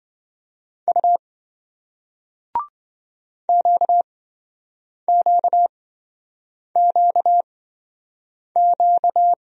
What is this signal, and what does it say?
Morse code, heard as a keyed tone, about 27 wpm, spaced as for 7 wpm, 700 Hz. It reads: U QQQQ